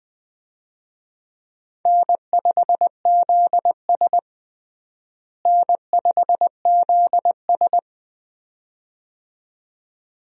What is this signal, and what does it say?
Morse code, heard as a keyed tone, about 20 wpm, 700 Hz. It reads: N5ZS N5ZS
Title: USA Call Signs - 4 Characters - Repeated 2x - 20wpm